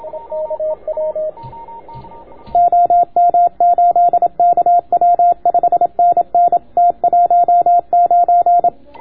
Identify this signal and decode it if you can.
Morse code, heard as a keyed tone, about 27 wpm, 700 Hz, with radio noise. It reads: OM8KW 5NN T19